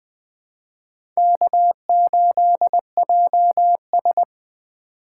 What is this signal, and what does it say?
Morse code, heard as a keyed tone, about 20 wpm, 700 Hz. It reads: K8JS